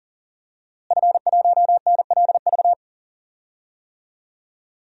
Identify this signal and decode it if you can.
Morse code, heard as a keyed tone, about 40 wpm, 700 Hz. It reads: F1NLV
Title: DXCC-100 Call Signs - 5 Characters - Repeated 3x - 40wpm